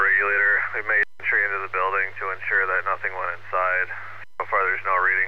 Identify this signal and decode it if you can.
Emergency Alert System (SAME) tones, heard as harsh buzzing bursts, with speech. It is heard from 1.2 to 4.2 s.